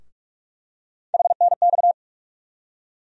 Morse code, heard as a keyed tone, about 45 wpm, 700 Hz. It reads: HNX
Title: USA Call Sign Postfixes - 3 Letters 45wpm